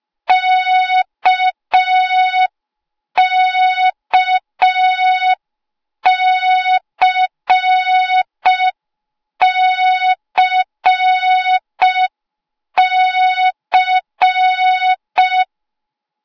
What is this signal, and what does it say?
Morse code, heard as a keyed tone, about 5 wpm, 3700 Hz. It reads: KKCCC